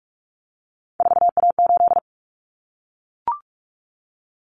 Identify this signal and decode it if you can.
Morse code, heard as a keyed tone, about 44 wpm, 700 Hz. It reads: 4A8
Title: Common Cut Numbers - 3-Digit 45wpm